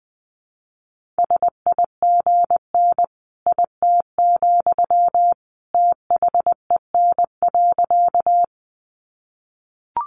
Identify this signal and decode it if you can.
Morse code, heard as a keyed tone, about 19 wpm, 700 Hz. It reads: SIGN IT, THEN.
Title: Sentences from Top 1000 Words 20wpm